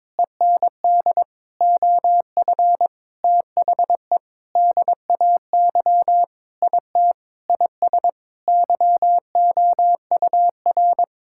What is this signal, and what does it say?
Morse code, heard as a keyed tone, about 22 wpm, 700 Hz. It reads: END OF THE DAY IT IS YOUR
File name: practice / 22wpm / Sentences with 8-Word Phrases - 22wpm